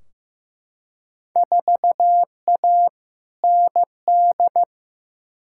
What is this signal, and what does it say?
Morse code, heard as a keyed tone, about 15 wpm, 700 Hz. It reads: 4A ND
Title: Field Day Exchanges Encore - 15wpm